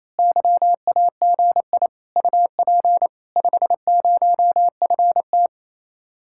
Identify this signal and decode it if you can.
Morse code, heard as a keyed tone, about 26 wpm, 700 Hz. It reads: YAGI UP 50FT
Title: Speed-Racing Common QSO Phrases - 28wpm